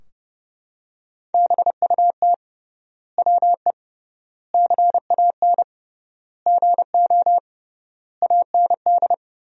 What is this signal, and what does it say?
Morse code, heard as a keyed tone, about 30 wpm, 700 Hz. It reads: BUT WE CAN GO AND
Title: 3x Word-Spacing - Sentences from Top 100 Words - 30wpm